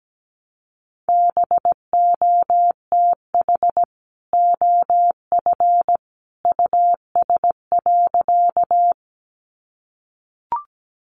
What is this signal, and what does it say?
Morse code, heard as a keyed tone, about 17 wpm, 700 Hz. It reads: BOTH OF US.
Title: Sentences from Top 300 Words 17wpm